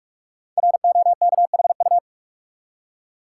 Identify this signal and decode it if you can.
Morse code, heard as a keyed tone, about 44 wpm, 700 Hz. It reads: ROKHU
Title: Code Groups - 5 Letters - 45wpm